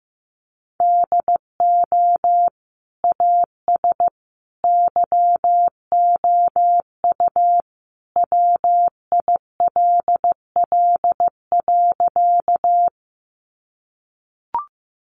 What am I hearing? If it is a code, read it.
Morse code, heard as a keyed tone, about 15 wpm, 700 Hz. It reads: DO AS YOU WILL.
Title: Sentences from Top 100 Words 15wpm